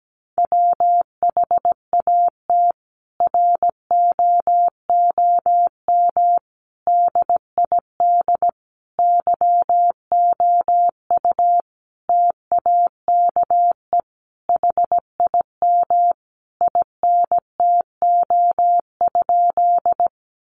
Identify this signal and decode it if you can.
Morse code, heard as a keyed tone, about 17 wpm, 700 Hz. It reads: WHAT ROOM DID YOU TAKE HIM INTO?
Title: Sentences from Top 600 Words 17wpm